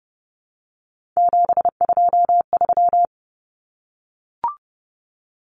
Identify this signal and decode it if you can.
Morse code, heard as a keyed tone, about 30 wpm, 700 Hz. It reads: 723
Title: Numbers - 3-Digit 30wpm